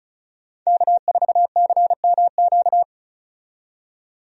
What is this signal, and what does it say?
Morse code, heard as a keyed tone, about 36 wpm, 700 Hz. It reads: K4CMQ
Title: USA Call Signs - 5 Characters - 35wpm